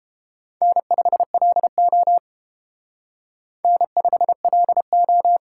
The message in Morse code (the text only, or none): N5LO N5LO